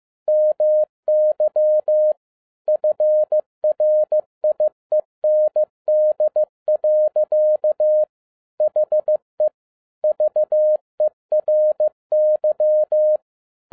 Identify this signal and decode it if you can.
Morse code, heard as a keyed tone, about 15 wpm, 610 Hz. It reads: MY FRIEND. HE VERY
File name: 2011-10-mp3 / n07-A DOT 15wpm.mp3